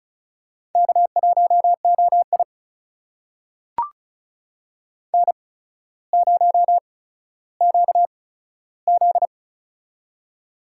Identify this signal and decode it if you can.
Morse code, heard as a keyed tone, about 36 wpm, spaced as for 15 wpm, 700 Hz. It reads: K1OI N0QZ
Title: USA Call Signs - 4 Characters - 8x Character Spacing - 35wpm